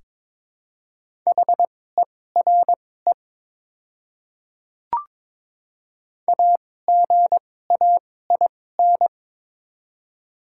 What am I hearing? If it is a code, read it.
Morse code, heard as a keyed tone, about 22 wpm, spaced as for 16 wpm, 700 Hz. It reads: HERE AGAIN
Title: 2x Character Spacing - Top 200 Words - 22wpm